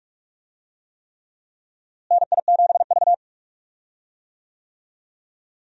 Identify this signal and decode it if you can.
Morse code, heard as a keyed tone, about 46 wpm, 700 Hz. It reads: NI7V
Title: USA Call Signs - 4 Characters - Repeated 3x - 45wpm